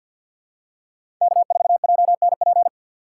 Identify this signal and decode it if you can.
Morse code, heard as a keyed tone, about 51 wpm, 700 Hz. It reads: K4JNP
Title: USA Call Signs - 3x Character Spacing - 50wpm